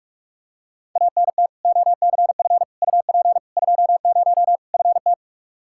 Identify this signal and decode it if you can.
Morse code, heard as a keyed tone, about 46 wpm, 700 Hz. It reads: ANT OCF UP 20 FT
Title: Practice QSO - Exchange 2 - Info - 45wpm